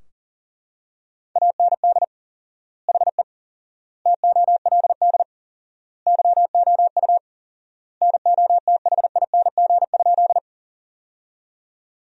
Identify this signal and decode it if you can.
Morse code, heard as a keyed tone, about 40 wpm, 700 Hz. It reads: AND HE TOLD YOU NOTHING?
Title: 4x Word-Spacing - Sentences from Top 500 Words - 40wpm